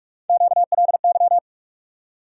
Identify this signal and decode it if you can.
Morse code, heard as a keyed tone, about 47 wpm, 700 Hz. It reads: QLY
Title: USA Call Sign Postfixes - 3 Letters 45wpm